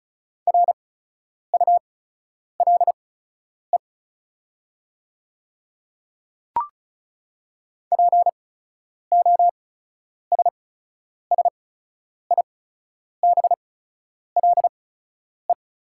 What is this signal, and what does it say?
Morse code, heard as a keyed tone, about 36 wpm, spaced as for 10 wpm, 700 Hz. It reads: RULE POSSIBLE
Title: 8x Character Spacing - Top 500 Words - 35wpm